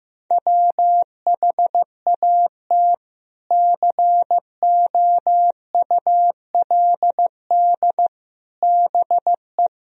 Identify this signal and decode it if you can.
Morse code, heard as a keyed tone, about 15 wpm, 700 Hz. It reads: WHAT COULD BE
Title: Sentences from Top 400 Words and Abbreviations 15wpm